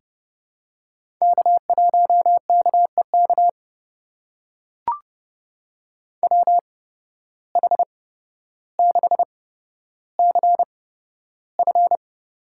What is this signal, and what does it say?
Morse code, heard as a keyed tone, about 30 wpm, spaced as for 13 wpm, 700 Hz. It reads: K1KEK WH6CF
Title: USA Call Signs - 5 Characters - 8x Character Spacing - 30wpm